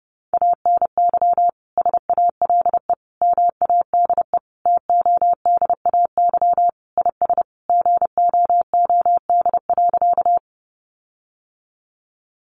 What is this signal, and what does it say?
Morse code, heard as a keyed tone, about 30 wpm, 700 Hz. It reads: ANY SALE MADE TODAY IS GOOD.
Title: Sentences from Top 1400 Words 30wpm